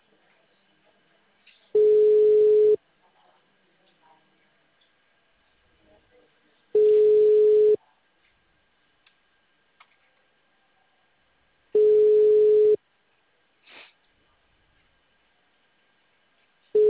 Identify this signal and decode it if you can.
Ringback tone (Europe, 425 Hz).
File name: out- -2026- - - .74